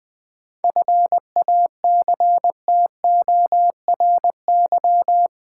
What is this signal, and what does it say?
Morse code, heard as a keyed tone, about 20 wpm, 700 Hz. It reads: FACTORY
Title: 7 Letter Words - First 3 Letters Given - 20wpm